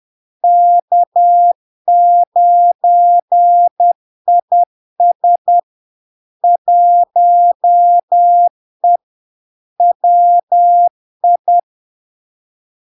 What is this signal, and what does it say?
Morse code, heard as a keyed tone, about 10 wpm, 700 Hz. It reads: K9IS 1E WI
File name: LARK-FD-2022-Log-in-CW-10wpm-700-Hz